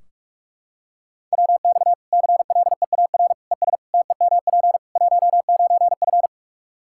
Morse code, heard as a keyed tone, about 46 wpm, 700 Hz. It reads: WX CLEAR ES TEMP 19F
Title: Practice QSO - Exchange 2 - Info - 45wpm